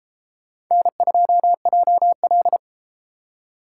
Morse code, heard as a keyed tone, about 33 wpm, 700 Hz. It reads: N2JL